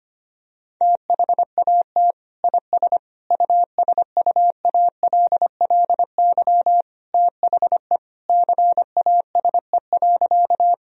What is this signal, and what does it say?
Morse code, heard as a keyed tone, about 25 wpm, 700 Hz. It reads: THAT IS USUALLY THE CASE.
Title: Sentences from Top 1000 Words - Review Part 3 - 25wpm